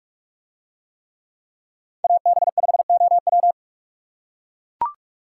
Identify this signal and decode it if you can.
Morse code, heard as a keyed tone, about 44 wpm, 700 Hz. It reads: AB5OW